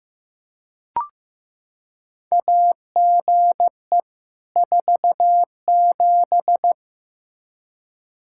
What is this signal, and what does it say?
Morse code, heard as a keyed tone, about 15 wpm, 700 Hz. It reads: AGE 47
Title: K1USN SST QSO Practice 15wpm